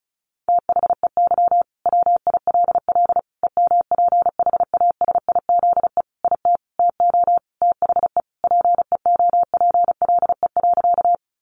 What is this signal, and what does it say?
Morse code, heard as a keyed tone, about 34 wpm, 700 Hz. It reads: THEY WILL EMPHASIZE IT TO THE PEOPLE.
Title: Sentences from Top 1500 Words 35wpm